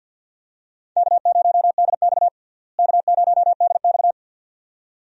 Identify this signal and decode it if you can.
Morse code, heard as a keyed tone, about 52 wpm, 700 Hz. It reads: K0DX K0DX
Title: USA Call Signs - 4 Characters - Repeated 2x - 50wpm